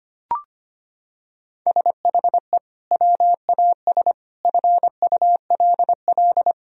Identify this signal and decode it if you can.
Morse code, heard as a keyed tone, about 25 wpm, 700 Hz. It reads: SHE WAS FULL
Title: Sentences from Top 1000 Words - Review Part 3 - 25wpm